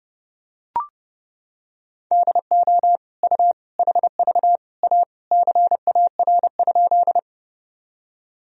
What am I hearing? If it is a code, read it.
Morse code, heard as a keyed tone, about 30 wpm, 700 Hz. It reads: DO U HV A CAR?
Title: Sentences from Top 300 Words and Abbreviations 30wpm